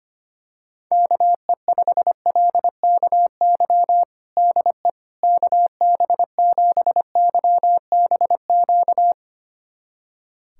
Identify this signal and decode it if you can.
Morse code, heard as a keyed tone, about 25 wpm, 700 Hz. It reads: KE5LKY DE KB7YBQ
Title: USA Callsign DE Callsign - 6 Characters - 25wpm